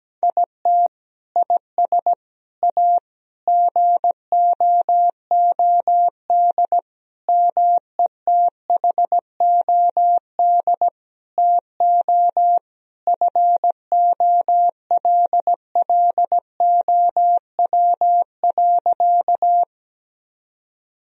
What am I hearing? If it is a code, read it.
Morse code, heard as a keyed tone, about 17 wpm, 700 Hz. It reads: IT IS A GOOD METHOD TO FOLLOW.
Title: Sentences from Top 1000 Words - Review Part 3 - 17wpm